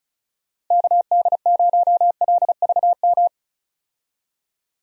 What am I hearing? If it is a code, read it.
Morse code, heard as a keyed tone, about 36 wpm, 700 Hz. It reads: KD0LVM
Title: USA Call Signs - 6 Characters - 35wpm